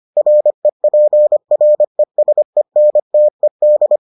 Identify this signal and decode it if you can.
Morse code, heard as a keyed tone, about 25 wpm, 600 Hz. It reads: REPRESENTED